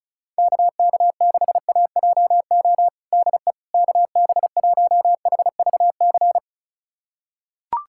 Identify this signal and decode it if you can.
Morse code, heard as a keyed tone, about 34 wpm, 690 Hz. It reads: KK6AJO DE KB1HVC
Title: USA Callsign DE Callsign 35wpm